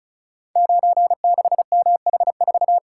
Morse code, heard as a keyed tone, about 36 wpm, 700 Hz. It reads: 96MH4
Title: Code Groups - 5 Letter:Numbers - 35wpm